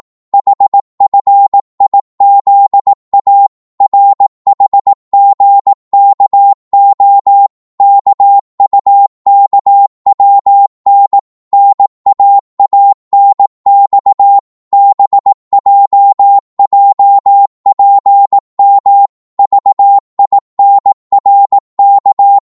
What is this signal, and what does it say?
Morse code, heard as a keyed tone, about 18 wpm, 800 Hz. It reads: HFIZA RHGKO KUKWN NAANX BJJPM VINRK
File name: Morse / ltr-18wpm